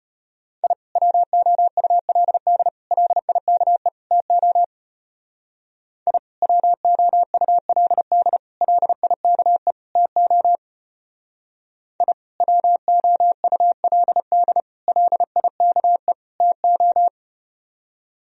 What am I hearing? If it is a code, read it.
Morse code, heard as a keyed tone, about 33 wpm, 700 Hz. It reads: I WOULD LIKE TO I WOULD LIKE TO I WOULD LIKE TO